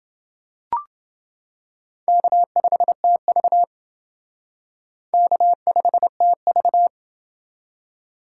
Morse code, heard as a keyed tone, about 29 wpm, 700 Hz. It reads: K5TV K5TV